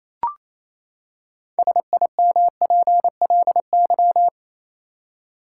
Morse code, heard as a keyed tone, about 27 wpm, 700 Hz. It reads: SIMPLY
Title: Six Letter Words - 28wpm